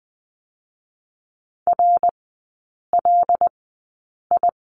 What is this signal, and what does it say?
Morse code, heard as a keyed tone, about 20 wpm, spaced as for 8 wpm, 700 Hz. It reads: RLI
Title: 145 Lesson 24e - Letter y - Three Characters 20wpm